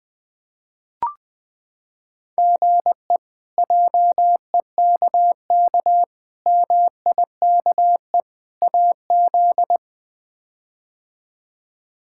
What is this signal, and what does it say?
Morse code, heard as a keyed tone, about 20 wpm, 700 Hz. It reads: GE JEKK MIKE AZ
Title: K1USN SST QSO Practice - 20wpm